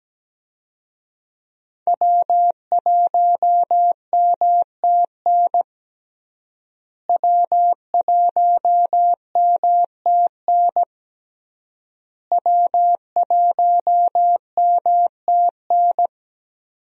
Morse code, heard as a keyed tone, about 17 wpm, 700 Hz. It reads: W1MTN W1MTN W1MTN
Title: USA Call Signs - 5 Characters - Repeated 3x - 17wpm